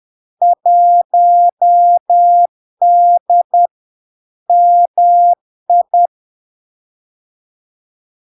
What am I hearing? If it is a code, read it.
Morse code, heard as a keyed tone, about 10 wpm, 700 Hz. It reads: 1D MI
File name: LARK-FD-2022-Log-in-CW-10wpm-700-Hz